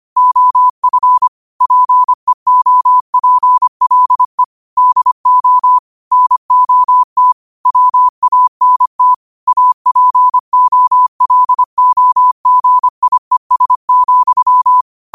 Morse code, heard as a keyed tone, about 25 wpm, 1000 Hz. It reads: OF PEOPLE DO NOT WANT APOLOGIES,